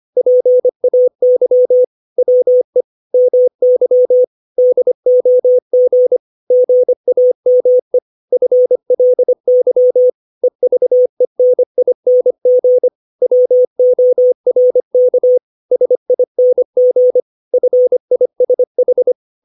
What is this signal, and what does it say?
Morse code, heard as a keyed tone, about 25 wpm, 500 Hz. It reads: PAY WE MY DOG GAME FLY EVENING WORK SING FISH
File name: 632 words / 632 Words 25WPM